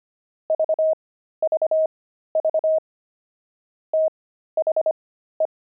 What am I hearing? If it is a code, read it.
Morse code, heard as a keyed tone, about 25 wpm, spaced as for 13 wpm, 630 Hz. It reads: VVV THE
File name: pangram-fs-13-cs-25-explorer